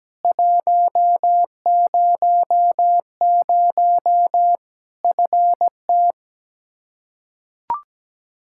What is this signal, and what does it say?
Morse code, heard as a keyed tone, about 17 wpm, 700 Hz. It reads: 100 FT
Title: Practice QSO - Exchange 2 - Info - 17wpm